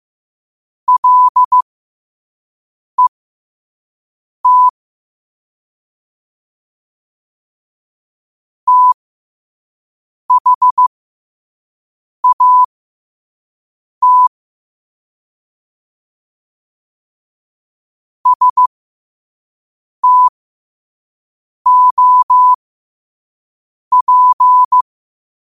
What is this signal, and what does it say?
Morse code, heard as a keyed tone, about 15 wpm, spaced as for 5 wpm, 1000 Hz. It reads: LET THAT STOP
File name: 5wpm